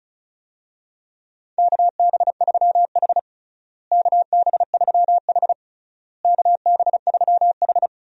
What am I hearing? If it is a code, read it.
Morse code, heard as a keyed tone, about 34 wpm, 700 Hz. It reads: KB3H KB3H KB3H
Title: USA Call Signs - 4 Characters - Repeated 3x - 35wpm